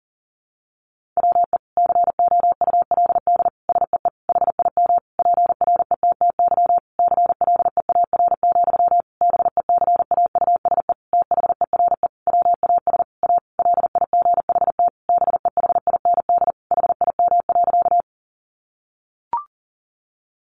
Morse code, heard as a keyed tone, about 40 wpm, 700 Hz. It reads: WE COULD SEE HIM PRETTY CLEAR, BECAUSE THERE WAS A LIGHT BEHIND HIM.
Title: Sentences from Top 1000 Words 40wpm